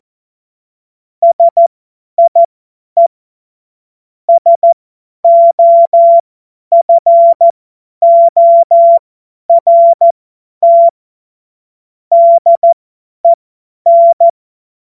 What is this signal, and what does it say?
Morse code, heard as a keyed tone, about 14 wpm, spaced as for 10 wpm, 680 Hz. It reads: SIE SOFORT DEN
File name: Morsekurs / Witze / Witz 29 April 2025 50 F70